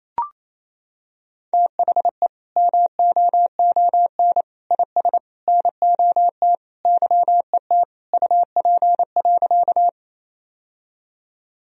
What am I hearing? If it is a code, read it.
Morse code, heard as a keyed tone, about 28 wpm, 700 Hz. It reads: THE MOON IS NOT YET UP.
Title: Sentences from Top 1000 Words - Review Part 3 - 28wpm